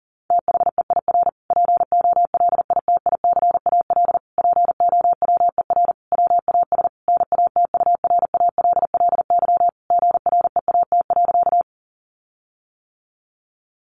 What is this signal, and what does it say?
Morse code, heard as a keyed tone, about 40 wpm, 700 Hz. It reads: THEIR POLITICAL POWER WAS NATURALLY GREAT.
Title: Sentences from Top 1300 Words 40wpm